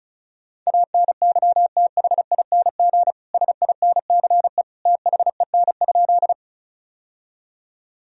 Morse code, heard as a keyed tone, about 34 wpm, 700 Hz. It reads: ANYTHING SINCE THEN?
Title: Sentences from Top 1000 Words - Review Part 1 - 35wpm